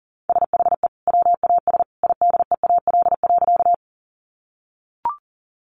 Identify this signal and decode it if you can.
Morse code, heard as a keyed tone, about 37 wpm, 700 Hz. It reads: SHE WAS IDEAL.
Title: Sentences from Top 1400 Words 40wpm